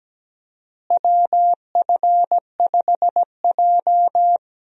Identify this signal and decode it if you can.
Morse code, heard as a keyed tone, about 17 wpm, 700 Hz. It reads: WF5J
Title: USA Call Signs - 4 Characters - Repeated 2x - 17wpm